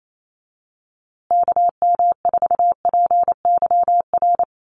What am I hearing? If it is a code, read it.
Morse code, heard as a keyed tone, about 28 wpm, 700 Hz. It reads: KM4PYR
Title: USA Call Signs Repeated 2x 28wpm